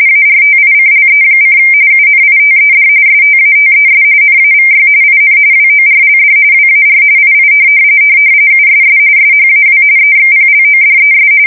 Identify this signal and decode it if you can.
RTTY, about 45.45 baud. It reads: ARTICLE: UNITED STATES NAVY IN WORLD WAR II USS SHAW EXPLODES
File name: fsk170 / usnavy4